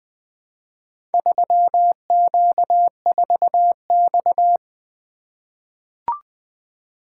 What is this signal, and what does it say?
Morse code, heard as a keyed tone, about 20 wpm, 700 Hz. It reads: 3Q4X
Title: Four Letter-Number Combinations - 20wpm